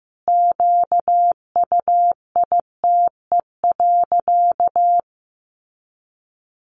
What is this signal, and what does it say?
Morse code, heard as a keyed tone, about 15 wpm, 700 Hz. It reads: QUITE.
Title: Sentences from Top 1300 Words 15wpm